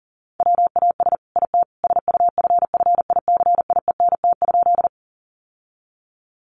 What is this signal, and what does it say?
Morse code, heard as a keyed tone, about 37 wpm, 700 Hz. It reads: WAS IT SUFFICIENT?
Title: Sentences from Top 1400 Words 40wpm